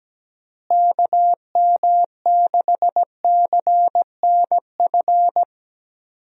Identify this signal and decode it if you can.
Morse code, heard as a keyed tone, about 17 wpm, 700 Hz. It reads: KM6CNF
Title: USA Call Signs - 6 Characters - Repeated 2x - 17wpm